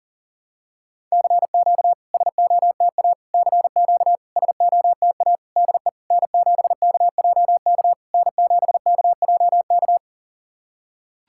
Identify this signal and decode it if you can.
Morse code, heard as a keyed tone, about 44 wpm, 660 Hz. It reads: CQ SOTA CQ SOTA DE N7KJK N7KJK
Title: SOTA QSO Practice 40wpm